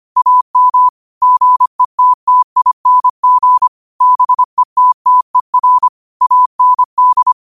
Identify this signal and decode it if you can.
Morse code, heard as a keyed tone, about 25 wpm, 1000 Hz. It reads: AM GETTING BETTER AND
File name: qotdfiles / 25wpm